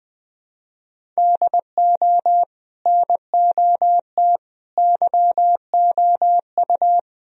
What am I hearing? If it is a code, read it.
Morse code, heard as a keyed tone, about 20 wpm, 700 Hz. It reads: DO NOT YOU
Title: Speed-Racing - Sets of 3 Words 17wpm